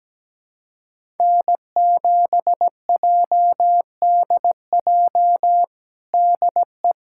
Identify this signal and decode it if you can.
Morse code, heard as a keyed tone, about 17 wpm, 700 Hz. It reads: N7JDJ DE
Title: USA Callsign DE Callsign 17wpm